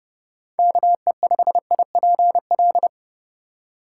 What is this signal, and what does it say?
Morse code, heard as a keyed tone, about 30 wpm, 700 Hz. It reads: KE5IPL